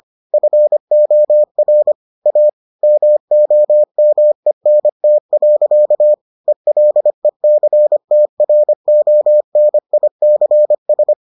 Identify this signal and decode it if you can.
Morse code, heard as a keyed tone, about 25 wpm, 600 Hz. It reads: FOR A MOMENT. ELECTRONICS